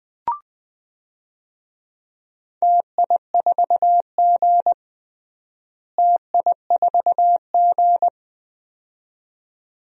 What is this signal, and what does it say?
Morse code, heard as a keyed tone, about 21 wpm, 700 Hz. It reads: TI4G TI4G
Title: DXCC-100 Call Signs - 4 Characters - Repeated 2x - 20wpm